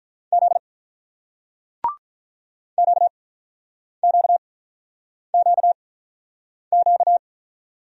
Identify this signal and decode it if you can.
Morse code, heard as a keyed tone, about 45 wpm, spaced as for 9 wpm, 700 Hz. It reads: Z QQQQ